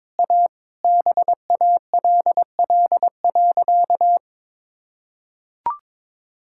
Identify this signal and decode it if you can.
Morse code, heard as a keyed tone, about 23 wpm, 700 Hz. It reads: A BALL.
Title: Sentences from Top 1000 Words - Review Part 6 - 22wpm